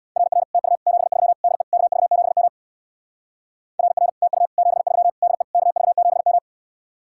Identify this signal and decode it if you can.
Morse code, heard as a keyed tone, about 62 wpm, 700 Hz. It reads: RR TU 73 DE KV7M RR TU 73 DE KV7M
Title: Speed-Racing Common QSO Phrases Encore - 50wpm